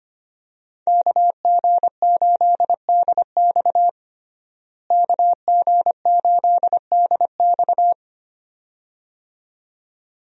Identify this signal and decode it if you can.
Morse code, heard as a keyed tone, about 25 wpm, 700 Hz. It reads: KG8DX KG8DX
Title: USA Call Signs - 5 Characters - Repeated 2x - 25wpm